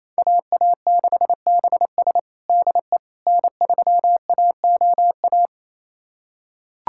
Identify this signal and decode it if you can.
Morse code, heard as a keyed tone, about 28 wpm, 700 Hz. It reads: AA6BS DE N3AOA
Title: USA Callsign DE Callsign - 5 Characters - 28wpm